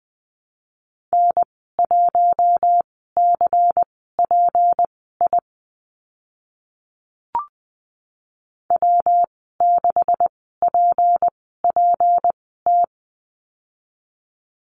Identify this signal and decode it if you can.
Morse code, heard as a keyed tone, about 20 wpm, spaced as for 14 wpm, 700 Hz. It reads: N1CPI W6PPT
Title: 193 Lesson 31g - Letter v - Call Signs 20wpm